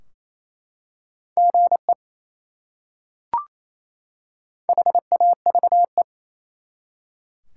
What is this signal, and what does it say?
Morse code, heard as a keyed tone, about 27 wpm, 700 Hz. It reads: GE HAVE
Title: Top 180 QSO Elements 28wpm